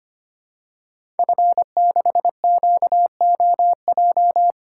Morse code, heard as a keyed tone, about 25 wpm, 700 Hz. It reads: F6QOJ
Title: Code Groups - 5 Letter:Numbers - 25wpm